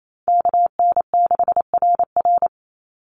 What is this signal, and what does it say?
Morse code, heard as a keyed tone, about 28 wpm, 700 Hz. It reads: KN6RR
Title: USA Call Signs 28wpm